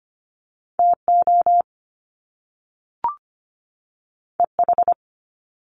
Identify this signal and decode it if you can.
Morse code, heard as a keyed tone, about 25 wpm, 700 Hz. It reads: TO EH
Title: Top Two Letter Combinations in Words 25wpm